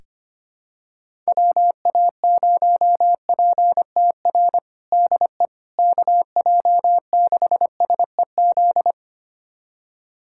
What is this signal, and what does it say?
Morse code, heard as a keyed tone, about 25 wpm, 700 Hz. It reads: WA0PTR DE KJ6SEZ
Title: USA Callsign DE Callsign - 6 Characters - 25wpm